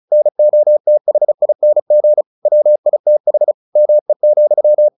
Morse code, heard as a keyed tone, about 34 wpm, 600 Hz. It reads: NOTHING WITH ME,
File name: the old man and the sea / the old man and the sea 1 35wpm